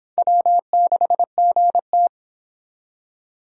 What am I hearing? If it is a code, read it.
Morse code, heard as a keyed tone, about 30 wpm, 700 Hz. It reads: W6GT